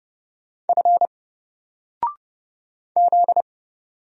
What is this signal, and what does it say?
Morse code, heard as a keyed tone, about 31 wpm, 700 Hz. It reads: F Z
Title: Single Letter-Number 30wpm